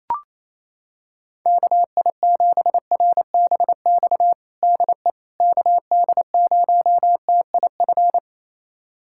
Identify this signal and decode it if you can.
Morse code, heard as a keyed tone, about 28 wpm, 700 Hz. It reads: KI7RBX DE KD0TIF
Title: USA Callsign DE Callsign - 6 Characters - 28wpm